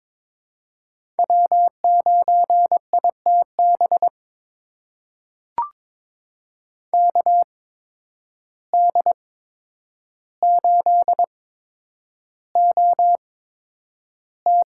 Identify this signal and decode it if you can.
Morse code, heard as a keyed tone, about 22 wpm, spaced as for 9 wpm, 700 Hz. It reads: W9ITB KD8OT
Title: USA Call Signs - 5 Characters - 8x Character Spacing - 22wpm